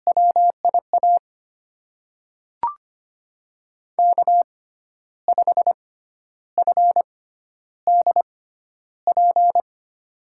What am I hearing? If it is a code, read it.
Morse code, heard as a keyed tone, about 25 wpm, spaced as for 11 wpm, 700 Hz. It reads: WIA K5FDP
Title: USA Call Signs - 5 Characters - 6x Character Spacing - 25wpm